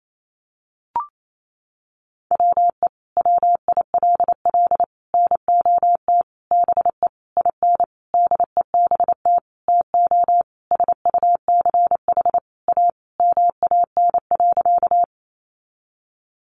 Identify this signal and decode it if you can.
Morse code, heard as a keyed tone, about 28 wpm, 700 Hz. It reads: WE WILL NOT BE IN DEBT TO SUCH A MAN.
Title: Sentences from Top 1300 Words 28wpm